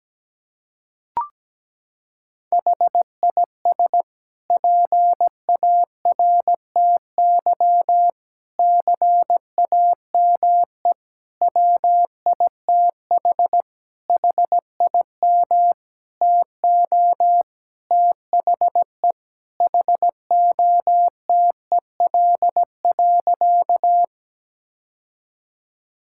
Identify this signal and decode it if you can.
Morse code, heard as a keyed tone, about 17 wpm, 700 Hz. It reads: HIS PARTY CAME WITH HIM TO THE HOTEL.
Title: Sentences from Top 1600 Words 17wpm